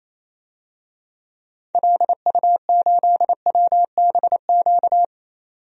Morse code, heard as a keyed tone, about 28 wpm, 700 Hz. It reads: LU8WBQ
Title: DXCC-100 Call Signs - 6 Characters - 28wpm